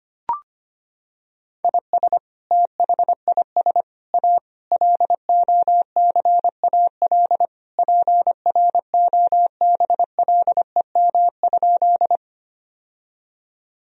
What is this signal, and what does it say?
Morse code, heard as a keyed tone, about 25 wpm, 700 Hz. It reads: IS THIS A LOCAL PROBLEM?
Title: Sentences from Top 1000 Words - Review Part 3 - 25wpm